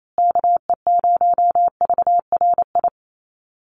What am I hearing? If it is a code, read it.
Morse code, heard as a keyed tone, about 28 wpm, 700 Hz. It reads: KE0VRI
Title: USA Call Signs Repeated 2x 28wpm